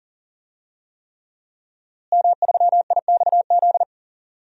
Morse code, heard as a keyed tone, about 40 wpm, 700 Hz. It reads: M3IXZ
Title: DXCC-100 Call Signs - 5 Characters - Repeated 3x - 40wpm